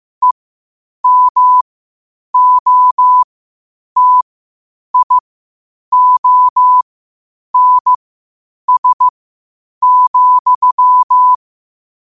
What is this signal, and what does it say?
Morse code, heard as a keyed tone, about 15 wpm, spaced as for 8 wpm, 1000 Hz. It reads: EMOTIONS,
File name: qotdfiles / 8wpm